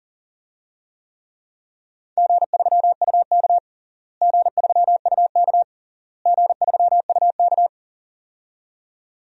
Morse code, heard as a keyed tone, about 40 wpm, 700 Hz. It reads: G3UK G3UK G3UK
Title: DXCC-100 Call Signs - 4 Characters - Repeated 3x - 40wpm